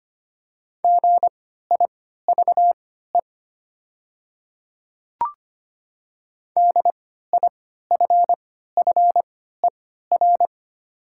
Morse code, heard as a keyed tone, about 25 wpm, spaced as for 14 wpm, 690 Hz. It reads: GIVE DIFFER
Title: 3x Character Spacing - Top 500 Words - 25wpm